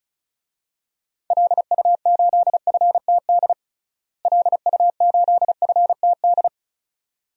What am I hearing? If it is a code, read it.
Morse code, heard as a keyed tone, about 34 wpm, 700 Hz. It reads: LU8FTD LU8FTD
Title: DXCC-100 Call Signs - 6 Characters - Repeated 2x - 35wpm